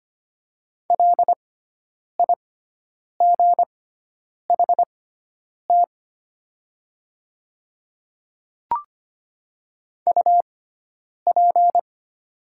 Morse code, heard as a keyed tone, about 25 wpm, spaced as for 9 wpm, 700 Hz. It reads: LIGHT UP
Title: 6x Character Spacing - Top 200 Words - 25wpm